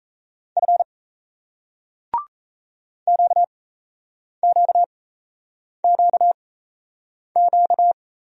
Morse code, heard as a keyed tone, about 37 wpm, spaced as for 8 wpm, 700 Hz. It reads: F QQQQ